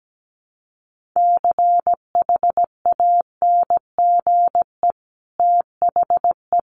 Morse code, heard as a keyed tone, about 17 wpm, 700 Hz. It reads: CHANGE THE
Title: Sentences from Top 800 Words 17wpm